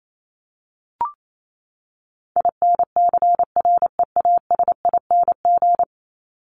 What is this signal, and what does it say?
Morse code, heard as a keyed tone, about 28 wpm, 700 Hz. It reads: INCREASING
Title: ING Suffix - Top 100 Words with Base Words 28wpm